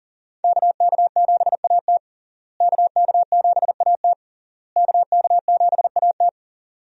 Morse code, heard as a keyed tone, about 40 wpm, 700 Hz. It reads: KK7AT KK7AT KK7AT
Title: USA Call Signs - 5 Characters - Repeated 3x - 40wpm